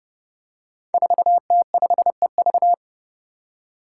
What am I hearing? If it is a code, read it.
Morse code, heard as a keyed tone, about 30 wpm, 700 Hz. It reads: 4T5EV
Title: Five Challenging Letter-Number Combinations 30wpm